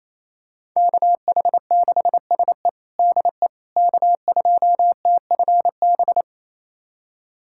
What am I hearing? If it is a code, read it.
Morse code, heard as a keyed tone, about 28 wpm, 700 Hz. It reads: KH6SE DE K2TFB
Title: USA Callsign DE Callsign - 5 Characters - 28wpm